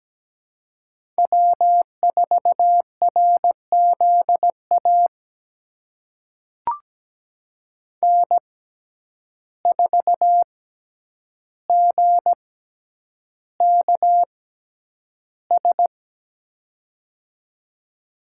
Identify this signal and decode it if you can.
Morse code, heard as a keyed tone, about 17 wpm, spaced as for 9 wpm, 700 Hz. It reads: W4RZA N4GKS
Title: USA Call Signs - 5 Characters - 6x Character Spacing - 17wpm